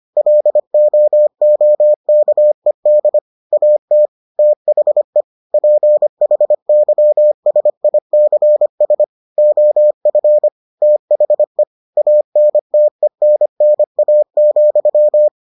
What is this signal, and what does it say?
Morse code, heard as a keyed tone, about 25 wpm, 600 Hz. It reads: LOOKED AT THE PHYSICS OF THE ANTENNA,